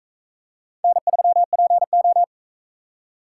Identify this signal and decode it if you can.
Morse code, heard as a keyed tone, about 42 wpm, 700 Hz. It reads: N3PO